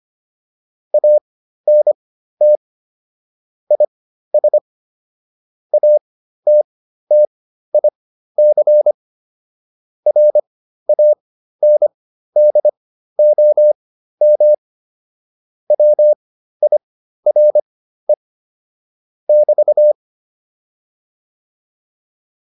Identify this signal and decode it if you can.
Morse code, heard as a keyed tone, about 25 wpm, spaced as for 13 wpm, 600 Hz. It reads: ANT IS ATTIC RANDOM WIRE <BT>